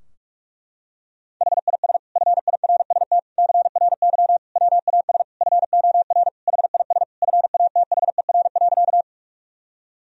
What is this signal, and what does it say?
Morse code, heard as a keyed tone, about 43 wpm, 700 Hz. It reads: HIS FIRST CRY WAS FOR HIS FATHER.
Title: Sentences from Top 1000 Words - Review Part 2 - 45wpm